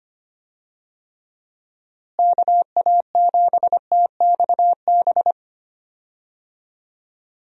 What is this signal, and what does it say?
Morse code, heard as a keyed tone, about 25 wpm, 700 Hz. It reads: KA7TXB
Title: USA Call Signs - 6 Characters - Repeated 2x - 25wpm